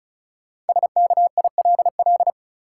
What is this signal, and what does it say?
Morse code, heard as a keyed tone, about 36 wpm, 700 Hz. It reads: SKILL
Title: Sentences from Top 1200 Words - New Word Drill Alt - 35wpm